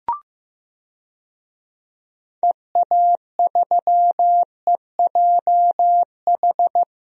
Morse code, heard as a keyed tone, about 15 wpm, 700 Hz. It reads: EA3EJH
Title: DXCC-100 Call Signs - 6 Characters - 15wpm